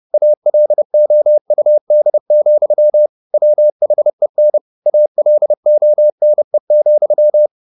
Morse code, heard as a keyed tone, about 30 wpm, 600 Hz. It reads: ALOUD, WHEN ALONE,
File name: the old man and the sea / the old man and the sea 1 30wpm